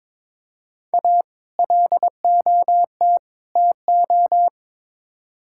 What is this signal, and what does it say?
Morse code, heard as a keyed tone, about 22 wpm, 700 Hz. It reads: A LOT TO
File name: practice / 22wpm / Build A Phrase - 5-Word Phrases - 22wpm